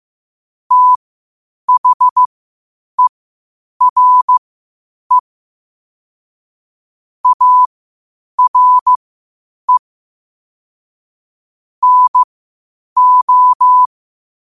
Morse code, heard as a keyed tone, about 15 wpm, spaced as for 8 wpm, 1000 Hz. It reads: THERE ARE NO
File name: qotdfiles / 8wpm